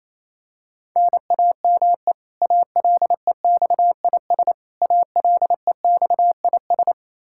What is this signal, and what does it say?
Morse code, heard as a keyed tone, about 28 wpm, 700 Hz. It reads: NAME ALEXIS ALEXIS
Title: Practice QSO - Exchange 1 - Beginning - 28wpm